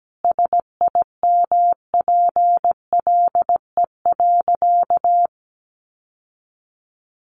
Morse code, heard as a keyed tone, about 17 wpm, 700 Hz. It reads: SIMPLE.
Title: Sentences from Top 700 Words 17wpm